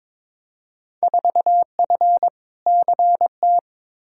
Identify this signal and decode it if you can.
Morse code, heard as a keyed tone, about 21 wpm, 700 Hz. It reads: 4F CT
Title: Field Day Exchanges Encore - 22wpm